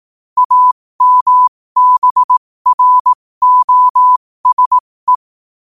Morse code, heard as a keyed tone, about 18 wpm, spaced as for 15 wpm, 1000 Hz. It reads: AMBROSE
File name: qotdfiles / 15wpm